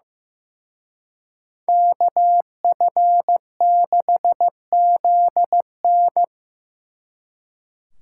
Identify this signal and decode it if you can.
Morse code, heard as a keyed tone, about 18 wpm, 700 Hz. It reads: KF6ZN
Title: Speed Racing - USA Call Signs 15wpm